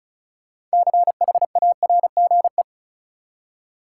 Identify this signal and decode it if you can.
Morse code, heard as a keyed tone, about 34 wpm, 700 Hz. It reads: CHARGE
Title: Six Letter Words - 35wpm